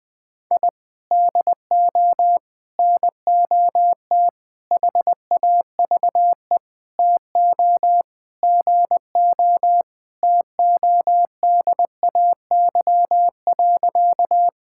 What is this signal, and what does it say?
Morse code, heard as a keyed tone, about 20 wpm, 700 Hz. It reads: I DO NOT HAVE TO GO TODAY.
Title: Sentences with 5-Word Phrases - 20wpm